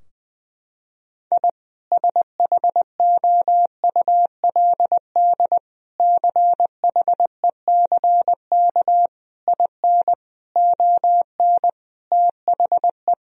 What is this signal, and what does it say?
Morse code, heard as a keyed tone, about 21 wpm, 700 Hz. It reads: I SHOULD CHECK IN ON THE
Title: Sentences from Top 1000 Words - Review Part 3 - 20wpm